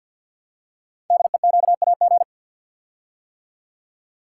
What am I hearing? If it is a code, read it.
Morse code, heard as a keyed tone, about 51 wpm, 700 Hz. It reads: DEQAG
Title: Code Groups - 5 Letters - 50wpm